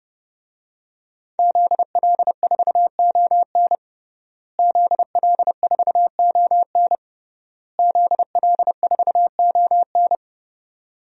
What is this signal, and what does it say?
Morse code, heard as a keyed tone, about 30 wpm, 700 Hz. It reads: ZL4ON ZL4ON ZL4ON
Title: DXCC-100 Call Signs - 5 Characters - Repeated 3x - 30wpm